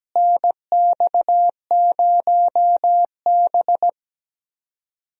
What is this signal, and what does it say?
Morse code, heard as a keyed tone, about 17 wpm, 700 Hz. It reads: NX0B
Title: USA Call Signs - 4 Characters - Repeated 3x - 17wpm